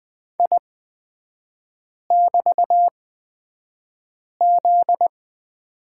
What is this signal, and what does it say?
Morse code, heard as a keyed tone, about 20 wpm, spaced as for 5 wpm, 700 Hz. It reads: I<BT>Z
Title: 243 Lesson 39e - Letter z - Three Characters 20wpm